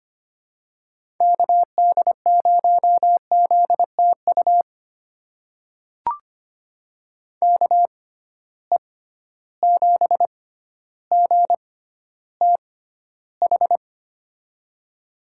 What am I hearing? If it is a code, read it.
Morse code, heard as a keyed tone, about 25 wpm, spaced as for 13 wpm, 700 Hz. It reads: KD0ZTU KE7GTH
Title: USA Call Signs - 6 Characters - 6x Character Spacing - 25wpm